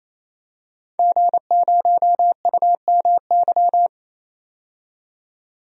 Morse code, heard as a keyed tone, about 28 wpm, 700 Hz. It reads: G0UMY
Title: DXCC-100 Call Signs - 5 Characters - 28wpm